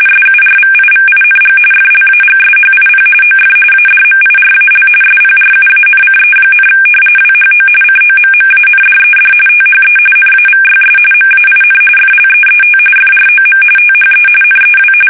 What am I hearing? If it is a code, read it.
RTTY, about 45.45 baud. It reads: SURVIVING FLAGSHIP OF ADMIRAL SPRUANCE, WHO COMMANDED THE NAVYS MAIN FLEET FROM THE NEW
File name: fsk850 / battleshipnj